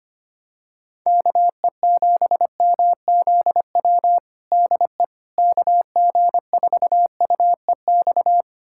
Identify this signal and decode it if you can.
Morse code, heard as a keyed tone, about 25 wpm, 700 Hz. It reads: KE7MZW DE KG4UEX
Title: USA Callsign DE Callsign - 6 Characters - 25wpm